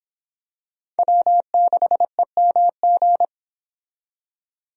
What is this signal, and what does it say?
Morse code, heard as a keyed tone, about 26 wpm, 700 Hz. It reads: W6EMG